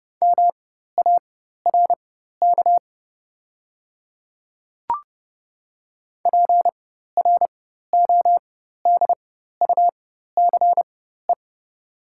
Morse code, heard as a keyed tone, about 30 wpm, spaced as for 14 wpm, 690 Hz. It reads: MARK PRODUCE
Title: 4x Character Spacing - Top 500 Words - 30wpm